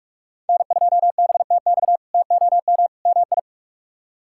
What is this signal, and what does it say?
Morse code, heard as a keyed tone, about 46 wpm, 700 Hz. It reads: N2BTX TOM MI
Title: K1USN SST QSO Practice - 45wpm